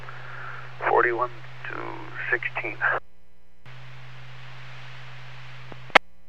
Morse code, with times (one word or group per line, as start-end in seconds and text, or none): none